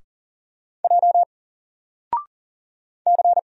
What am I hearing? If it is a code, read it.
Morse code, heard as a keyed tone, about 40 wpm, 700 Hz. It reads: J C
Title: Single Letter-Number Alt 40wpm